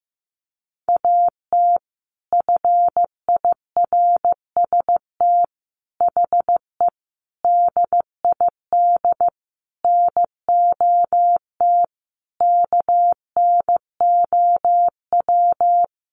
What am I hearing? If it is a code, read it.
Morse code, heard as a keyed tone, about 15 wpm, 700 Hz. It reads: AT FIRST HE DID NOT KNOW
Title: Sentences from Top 600 Words 15wpm